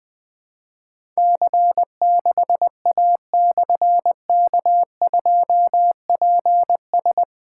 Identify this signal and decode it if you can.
Morse code, heard as a keyed tone, about 21 wpm, 700 Hz. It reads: C6A/K2PS
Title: Difficult Call Signs 20wpm